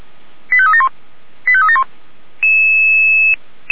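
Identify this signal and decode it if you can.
Five-tone selective call (ZVEI), heard as quick runs of five tones, tones of about 70 ms. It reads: 85371 85371
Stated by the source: ZVEI five-tone call 85381 received by SDR.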